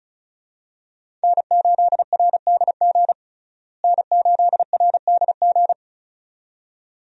Morse code, heard as a keyed tone, about 36 wpm, 700 Hz. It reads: N8RDG N8RDG
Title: USA Call Signs - 5 Characters - Repeated 2x - 35wpm